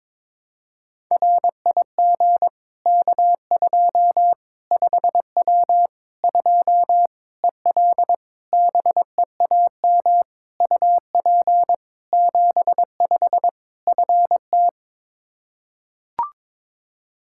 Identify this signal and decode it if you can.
Morse code, heard as a keyed tone, about 22 wpm, 700 Hz. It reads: RIG K2 5W 2 EL BEAM UP 75 FT
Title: Practice QSO - Exchange 2 - Info - 22wpm